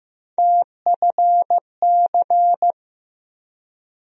Morse code, heard as a keyed tone, about 15 wpm, 700 Hz. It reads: TFC
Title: USA Call Sign Postfixes - 3 Letters 15wpm